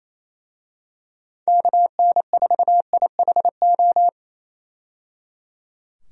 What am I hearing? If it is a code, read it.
Morse code, heard as a keyed tone, about 28 wpm, 700 Hz. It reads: KN4IHO
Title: USA Call Signs - 6 Characters - 28wpm